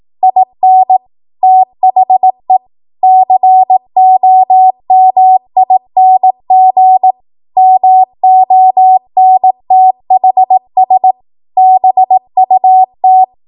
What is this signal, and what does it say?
Morse code, heard as a keyed tone, about 18 wpm, 750 Hz. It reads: IN THE COMING MONTHS BUT
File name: Morse / 18WPM